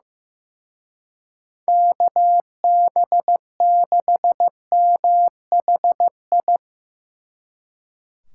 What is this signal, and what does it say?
Morse code, heard as a keyed tone, about 17 wpm, 700 Hz. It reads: KB6MHI
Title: Speed Racing - USA 2x3 Call Signs 15wpm